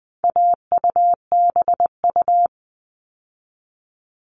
Morse code, heard as a keyed tone, about 20 wpm, 700 Hz. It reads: AUBU
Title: All Cut Numbers - 4-Digit 20wpm